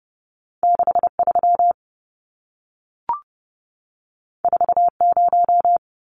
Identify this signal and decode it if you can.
Morse code, heard as a keyed tone, about 30 wpm, 700 Hz. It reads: 63 40
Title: Numbers - 2-Digit 30wpm